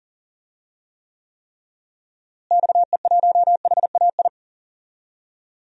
Morse code, heard as a keyed tone, about 40 wpm, 700 Hz. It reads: XE1HAI
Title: DXCC-100 Call Signs - 6 Characters - Repeated 3x - 40wpm